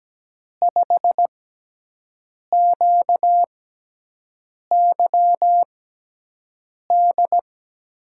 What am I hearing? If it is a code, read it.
Morse code, heard as a keyed tone, about 17 wpm, spaced as for 6 wpm, 700 Hz. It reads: 5QYD
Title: USA Call Signs - 6 Characters - 6x Character Spacing - 17wpm